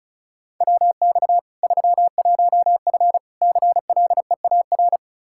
Morse code, heard as a keyed tone, about 36 wpm, 700 Hz. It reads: WX 31F CLEAR
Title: Practice QSO - Exchange 2 - Info - 35wpm